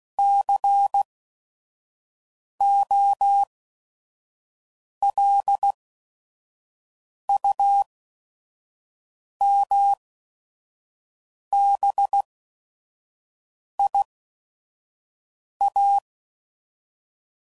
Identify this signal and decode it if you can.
Morse code, heard as a keyed tone, about 16 wpm, spaced as for 5 wpm, 800 Hz. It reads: COLUMBIA
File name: CW Code Groups @5WPM / fromSM's / Cities